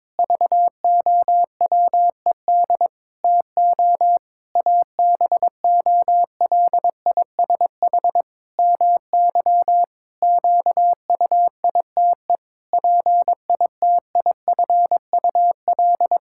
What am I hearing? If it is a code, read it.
Morse code, heard as a keyed tone, about 22 wpm, 700 Hz. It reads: VOWED TO ABOLISH MY QUITE PITIFUL
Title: Pangram Challenge v2 - 22wpm